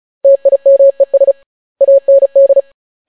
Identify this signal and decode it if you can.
Morse code, heard as a keyed tone, about 36 wpm, 550 Hz. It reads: TIMES AND